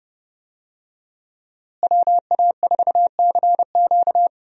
Morse code, heard as a keyed tone, about 31 wpm, 700 Hz. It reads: WA4CQ
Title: USA Call Signs - 5 Characters - Repeated 2x - 30wpm